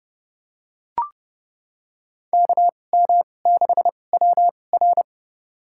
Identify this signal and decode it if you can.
Morse code, heard as a keyed tone, about 30 wpm, spaced as for 22 wpm, 700 Hz. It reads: KM6WR
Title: USA Call Signs - 5 Characters - 2x Character Spacing - 30wpm